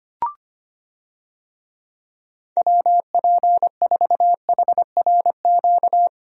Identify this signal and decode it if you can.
Morse code, heard as a keyed tone, about 25 wpm, 700 Hz. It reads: WP4HRQ
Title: USA Call Signs - 6 Characters - 25wpm